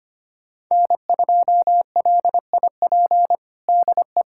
Morse code, heard as a keyed tone, about 25 wpm, 700 Hz. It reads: N2LIP DE
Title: USA Callsign DE Callsign - 5 Characters - 25wpm